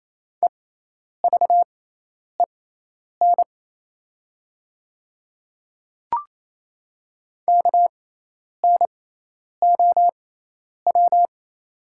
Morse code, heard as a keyed tone, about 28 wpm, spaced as for 10 wpm, 700 Hz. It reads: EVEN KNOW
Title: 6x Character Spacing - Top 200 Words - 28wpm